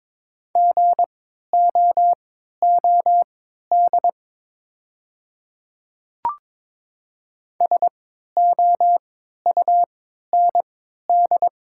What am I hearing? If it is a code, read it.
Morse code, heard as a keyed tone, about 22 wpm, spaced as for 12 wpm, 700 Hz. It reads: GOOD SOUND
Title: 3x Character Spacing - Top 200 Words - 22wpm